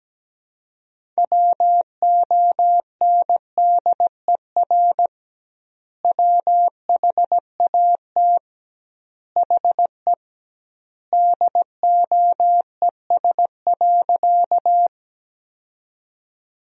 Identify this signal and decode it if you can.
Morse code, heard as a keyed tone, about 17 wpm, 700 Hz. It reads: WONDER WHAT HE DOES.
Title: 2x Word-Spacing - Sentences from Top 500 Words - 17wpm